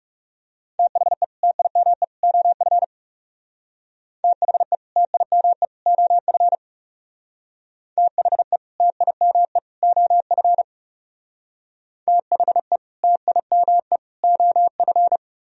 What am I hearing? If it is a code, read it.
Morse code, heard as a keyed tone, about 36 wpm, 700 Hz. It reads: THE TIME OF THE TIME OF THE TIME OF THE TIME OF
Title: Speed-Racing - Sets of 3 Words 30wpm